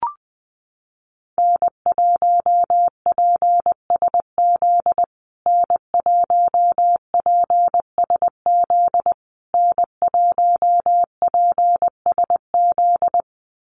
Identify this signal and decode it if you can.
Morse code, heard as a keyed tone, about 20 wpm, 700 Hz. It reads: N1PSZ N1PSZ N1PSZ
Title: USA Call Signs Repeated 3x 20wpm